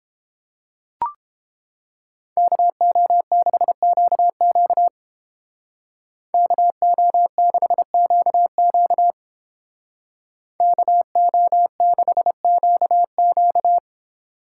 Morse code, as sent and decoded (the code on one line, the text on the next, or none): -.- --- -.... --.- --.- / -.- --- -.... --.- --.- / -.- --- -.... --.- --.-
KO6QQ KO6QQ KO6QQ